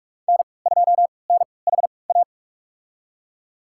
Morse code, heard as a keyed tone, about 46 wpm, spaced as for 25 wpm, 700 Hz. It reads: N2NHA
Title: USA Call Signs - 3x Character Spacing - 45wpm